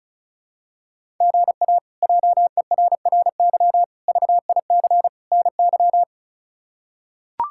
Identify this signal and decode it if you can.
Morse code, heard as a keyed tone, about 36 wpm, 700 Hz. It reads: GA JERRY VIC NY
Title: K1USN SST QSO Practice - 35wpm